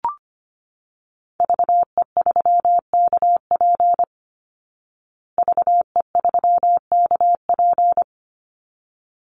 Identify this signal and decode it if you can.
Morse code, heard as a keyed tone, about 25 wpm, 700 Hz. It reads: VE3KP VE3KP
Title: Call Signs Repeated 3x - 25wpm